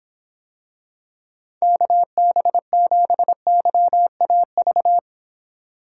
Morse code, heard as a keyed tone, about 26 wpm, 700 Hz. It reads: KB7YAV